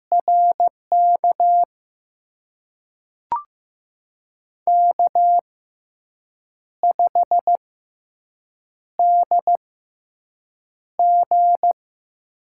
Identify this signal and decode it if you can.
Morse code, heard as a keyed tone, about 15 wpm, spaced as for 6 wpm, 700 Hz. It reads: RK K5DG
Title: USA Call Signs - 6x Character Spacing - 15wpm